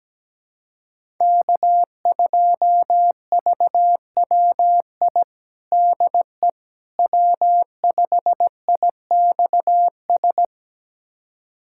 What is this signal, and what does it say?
Morse code, heard as a keyed tone, about 17 wpm, 700 Hz. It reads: K2VWI DE W5IXS
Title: USA Callsign DE Callsign - 5 Characters - 17wpm